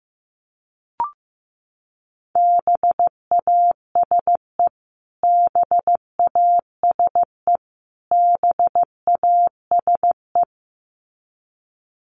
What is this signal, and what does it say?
Morse code, heard as a keyed tone, about 15 wpm, 700 Hz. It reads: BASE BASE BASE
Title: Top 500 Words Repeated 15wpm